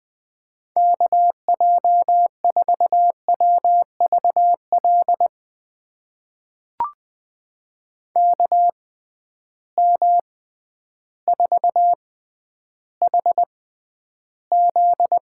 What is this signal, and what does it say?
Morse code, heard as a keyed tone, about 20 wpm, spaced as for 11 wpm, 700 Hz. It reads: KJ4WVL KM4HZ
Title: USA Call Signs - 6 Characters - 6x Character Spacing - 20wpm